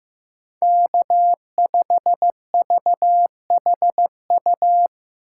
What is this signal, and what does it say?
Morse code, heard as a keyed tone, about 15 wpm, 700 Hz. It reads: K5VHU
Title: USA Call Signs - 2x Character Spacing - 15wpm